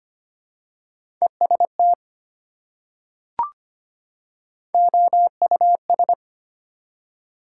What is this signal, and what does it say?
Morse code, heard as a keyed tone, about 25 wpm, 700 Hz. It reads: EST OUS
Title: Prefixes and Suffixes 25wpm